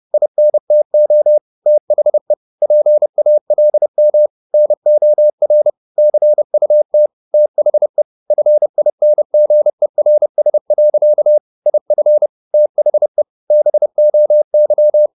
Morse code, heard as a keyed tone, about 30 wpm, 600 Hz. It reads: INTO THE PALM NOR CUT THE FINGERS. IF THE BOY